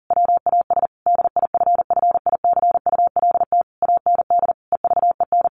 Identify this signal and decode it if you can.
Morse code, heard as a keyed tone, about 40 wpm, 700 Hz. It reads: WAS DIFFICULT AND EVEN
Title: Sentences from Top 1200 Words 40wpm